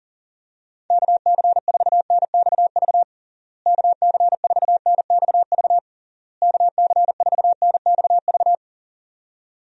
Morse code, heard as a keyed tone, about 39 wpm, 700 Hz. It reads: KC4NXV KC4NXV KC4NXV
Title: USA Call Signs - 6 Characters - Repeated 3x - 40wpm